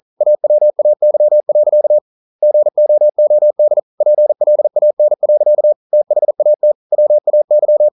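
Morse code, heard as a keyed tone, about 39 wpm, 600 Hz. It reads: AWAY. GOOD PLAN. THAT WAY
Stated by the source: Morse code practice at 41 wpm, from CW med Gnister.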